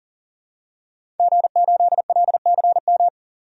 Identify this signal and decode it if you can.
Morse code, heard as a keyed tone, about 40 wpm, 700 Hz. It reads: G8LCM
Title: DXCC-100 Call Signs - 5 Characters - 40wpm